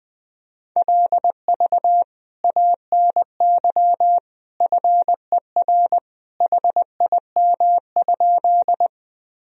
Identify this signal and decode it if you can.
Morse code, heard as a keyed tone, about 20 wpm, 700 Hz. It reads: LV ANY FER HIM?
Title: Sentences from Top 400 Words and Abbreviations 20wpm